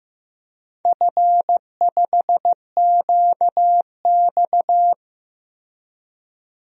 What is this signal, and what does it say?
Morse code, heard as a keyed tone, about 15 wpm, 700 Hz. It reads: F5QX
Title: DXCC-100 Call Signs - 4 Characters - Repeated 2x - 15wpm